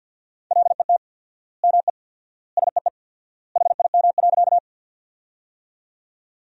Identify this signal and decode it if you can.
Morse code, heard as a keyed tone, about 50 wpm, 700 Hz. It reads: LET ME SEE HIM.
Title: 4x Word-Spacing - Sentences from Top 300 Words - 50wpm